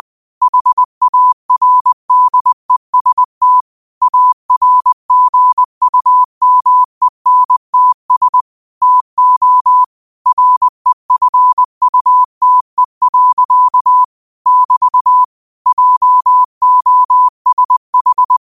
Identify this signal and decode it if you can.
Morse code, heard as a keyed tone, about 20 wpm, 1000 Hz. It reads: HARDEST ARGUMENTS TO REFUTE. <BT> JOSH
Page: New quotes every day in morse code at 20 Words per minute.